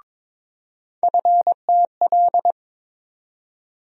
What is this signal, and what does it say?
Morse code, heard as a keyed tone, about 22 wpm, 700 Hz. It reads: FTL
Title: USA Call Sign Postfixes - 3 Letters 22wpm